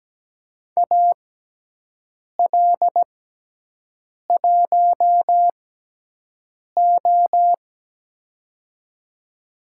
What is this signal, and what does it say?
Morse code, heard as a keyed tone, about 17 wpm, spaced as for 6 wpm, 700 Hz. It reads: AL1O
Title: USA Call Signs - 4 Characters - 6x Character Spacing - 17wpm